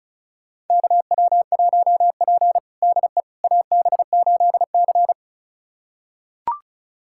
Morse code, heard as a keyed tone, about 36 wpm, 700 Hz. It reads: KW1P DE AB8C
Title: USA Callsign DE Callsign - 4 Characters - 35wpm